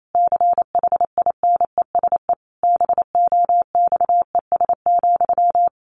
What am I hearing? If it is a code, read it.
Morse code, heard as a keyed tone, about 28 wpm, 700 Hz. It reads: CHINESE BOXES,
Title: A Sound Of Thunder 28wpm